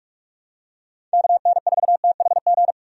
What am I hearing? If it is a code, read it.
Morse code, heard as a keyed tone, about 43 wpm, 700 Hz. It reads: KN4THG
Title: USA Call Signs - 6 Characters - 45wpm